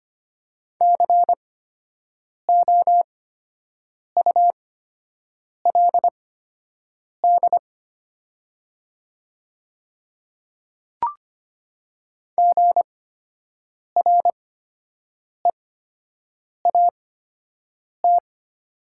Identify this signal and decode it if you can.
Morse code, heard as a keyed tone, about 25 wpm, spaced as for 7 wpm, 700 Hz. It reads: COULD GREAT
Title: 8x Character Spacing - Top 300 Words - 25wpm